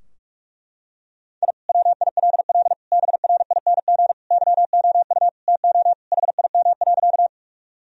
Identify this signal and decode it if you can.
Morse code, heard as a keyed tone, about 41 wpm, 700 Hz. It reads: I WILL BRING YOU TO HIM.
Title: Sentences from Top 1000 Words - Review Part 3 - 45wpm